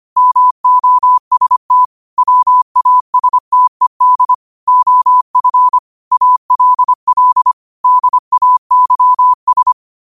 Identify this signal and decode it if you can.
Morse code, heard as a keyed tone, about 25 wpm, 1000 Hz. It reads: MOST WASTED OF ALL DAYS
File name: qotdfiles / 25wpm